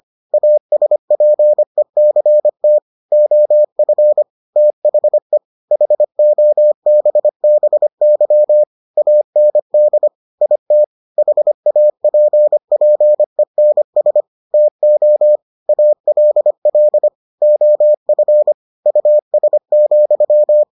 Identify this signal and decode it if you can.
Morse code, heard as a keyed tone, about 25 wpm, 600 Hz. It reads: ASPECT OF THE HOBBY AND IT HAPPENS TO ALL OF US,